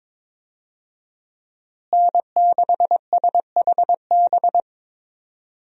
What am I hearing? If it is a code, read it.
Morse code, heard as a keyed tone, about 22 wpm, 700 Hz. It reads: N6SHB
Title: USA Call Signs - 5 Characters - Repeated 2x - 22wpm